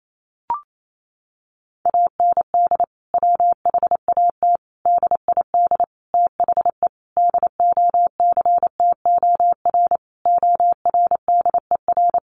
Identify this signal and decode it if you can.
Morse code, heard as a keyed tone, about 28 wpm, 700 Hz. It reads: AND WHAT DID THE DOCTOR ORDER
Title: Sentences from Top 1500 Words 28wpm